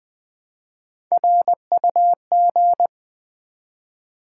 Morse code, heard as a keyed tone, about 20 wpm, 690 Hz. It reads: RUG
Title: Build A Word - 1 Letter At A Time - 5-Letter Words - 20wpm